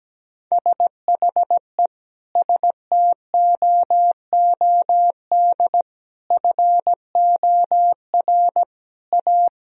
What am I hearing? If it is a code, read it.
Morse code, heard as a keyed tone, about 17 wpm, 700 Hz. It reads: SHE STOOD FOR A
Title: Sentences from Top 1000 Words - Review Part 3 - 17wpm